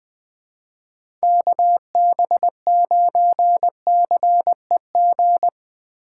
Morse code, heard as a keyed tone, about 20 wpm, 700 Hz. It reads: KB9CEG
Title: USA Call Signs - 4x Character Spacing - 20wpm